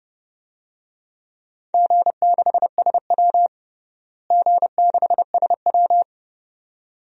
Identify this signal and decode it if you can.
Morse code, heard as a keyed tone, about 30 wpm, 700 Hz. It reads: G6SW G6SW
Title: DXCC-100 Call Signs - 4 Characters - Repeated 2x - 30wpm